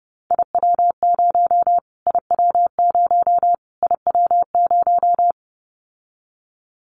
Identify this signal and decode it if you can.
Morse code, heard as a keyed tone, about 30 wpm, 700 Hz. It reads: IW0 IW0 IW0
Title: Callsign Prefixes for Top 25 DXCC - Repeated 3x - 30wpm